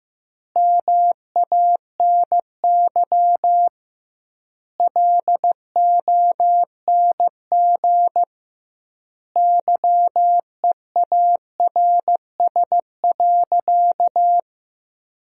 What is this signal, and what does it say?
Morse code, heard as a keyed tone, about 15 wpm, 700 Hz. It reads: MANY LONG YEARS.
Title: 2x Word-Spacing - Sentences from Top 200 Words - 15wpm